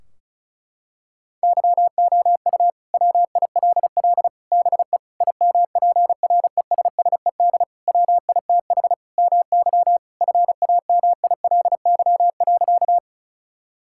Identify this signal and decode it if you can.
Morse code, heard as a keyed tone, about 36 wpm, 700 Hz. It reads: YOU WILL BE IMPRESSED WITH MY FAMILY.
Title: Sentences from Top 1600 Words 35wpm